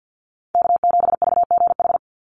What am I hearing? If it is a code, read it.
Morse code, heard as a keyed tone, about 50 wpm, 700 Hz. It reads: K7VGH
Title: USA Call Signs 50wpm